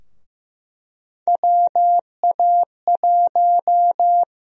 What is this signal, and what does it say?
Morse code, heard as a keyed tone, about 15 wpm, 700 Hz. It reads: WA1
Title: USA Call Sign Prefixes - 15wpm